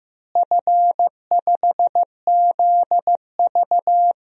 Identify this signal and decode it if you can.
Morse code, heard as a keyed tone, about 15 wpm, 700 Hz. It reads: F5ZV
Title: DXCC-100 Call Signs - 4 Characters - Repeated 2x - 15wpm